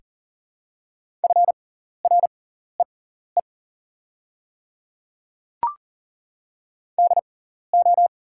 Morse code, heard as a keyed tone, about 41 wpm, spaced as for 14 wpm, 700 Hz. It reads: FREE DO
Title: 6x Character Spacing - Top 500 Words - 40wpm